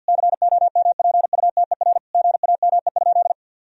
Morse code, heard as a keyed tone, about 52 wpm, 700 Hz. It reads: COMPUTER GAME?
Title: Sentences from Top 1000 Words - Review Part 3 - 50wpm